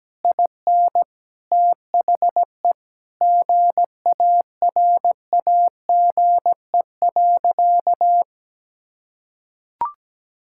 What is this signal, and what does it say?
Morse code, heard as a keyed tone, about 17 wpm, 700 Hz. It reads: IN THE GARAGE.
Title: Sentences from Top 1600 Words 17wpm